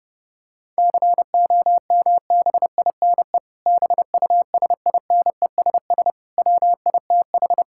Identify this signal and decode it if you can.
Morse code, heard as a keyed tone, about 32 wpm, 700 Hz. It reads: COMBINE BUSINESS WITH
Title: Sentences from Top 1600 Words 30wpm